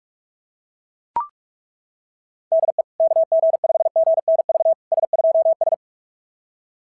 Morse code, heard as a keyed tone, about 38 wpm, 660 Hz. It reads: DE KG5GNV S2S
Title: SOTA QSO Practice 45wpm